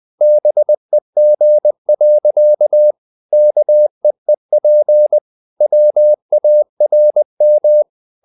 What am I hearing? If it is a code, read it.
Morse code, heard as a keyed tone, about 20 wpm, 600 Hz. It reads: BEG. KEEP WARM
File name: the old man and the sea / the old man and the sea part 1 20wpm